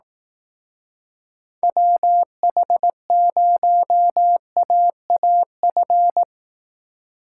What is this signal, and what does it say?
Morse code, heard as a keyed tone, about 18 wpm, 700 Hz. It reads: WH0AAF
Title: Speed Racing - USA Call Signs 15wpm